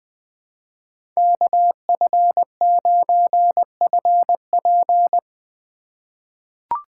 Morse code, heard as a keyed tone, about 21 wpm, 700 Hz. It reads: KF9FP
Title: USA Call Signs - 5 Characters - 2x Character Spacing - 20wpm